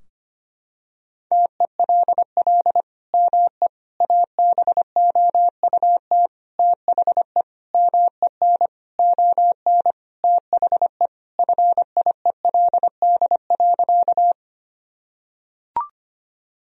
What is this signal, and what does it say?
Morse code, heard as a keyed tone, about 25 wpm, 700 Hz. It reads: TELL ME ABOUT THE MEN ON THE FIELD.
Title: Sentences from Top 1000 Words - Review Part 2 - 25wpm